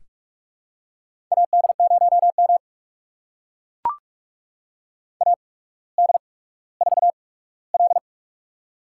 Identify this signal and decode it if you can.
Morse code, heard as a keyed tone, about 44 wpm, spaced as for 19 wpm, 700 Hz. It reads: AD0M AD4L
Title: USA Call Signs - 4 Characters - 8x Character Spacing - 45wpm